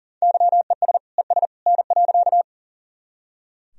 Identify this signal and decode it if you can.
Morse code, heard as a keyed tone, about 40 wpm, 700 Hz. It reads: YES ES N.
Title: Sentences from Top 500 Words and Abbreviations 40wpm